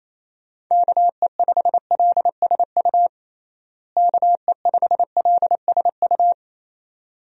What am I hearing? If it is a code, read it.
Morse code, heard as a keyed tone, about 28 wpm, 700 Hz. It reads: KE5LSU KE5LSU
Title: USA Call Signs - 6 Characters - Repeated 2x - 28wpm